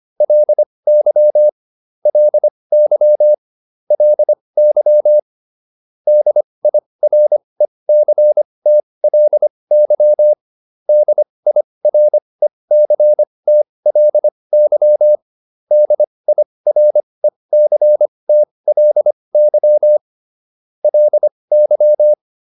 Morse code, heard as a keyed tone, about 25 wpm, spaced as for 20 wpm, 600 Hz. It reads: LY LY LY DIRECTLY DIRECTLY DIRECTLY LY